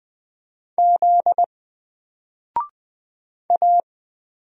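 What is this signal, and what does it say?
Morse code, heard as a keyed tone, about 20 wpm, 700 Hz. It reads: Z A
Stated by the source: Single letter and number Morse drill at 20 wpm, from N4LCD.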